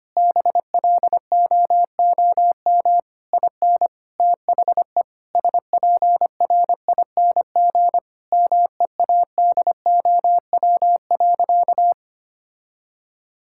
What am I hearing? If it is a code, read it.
Morse code, heard as a keyed tone, about 25 wpm, 700 Hz. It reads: BLOOM IN THE SPRING MEADOW.
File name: practice / 25wpm / Difficult Sentences with Hints - 25wpm